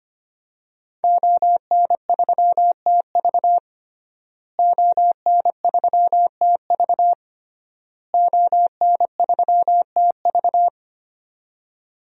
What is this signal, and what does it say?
Morse code, heard as a keyed tone, about 25 wpm, 700 Hz. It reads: ON3TV ON3TV ON3TV
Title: DXCC-100 Call Signs - 5 Characters - Repeated 3x - 25wpm